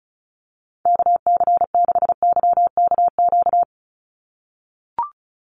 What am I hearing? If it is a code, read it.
Morse code, heard as a keyed tone, about 34 wpm, 700 Hz. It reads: KC6YKQ
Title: USA Call Signs 35wpm